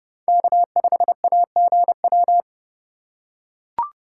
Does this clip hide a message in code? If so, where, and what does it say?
Yes: Morse code from 0.3 to 2.4 s, heard as a keyed tone, about 28 wpm, 700 Hz. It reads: K5AGW